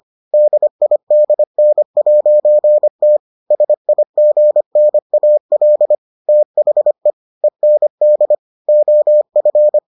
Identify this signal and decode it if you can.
Morse code, heard as a keyed tone, about 25 wpm, 600 Hz. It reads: DIDN'T SIGNAL THE END OF